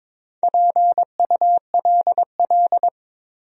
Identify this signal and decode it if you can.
Morse code, heard as a keyed tone, about 22 wpm, 700 Hz. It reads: PULL
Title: 3x Word-Spacing - Sentences from Top 400 Words - 22wpm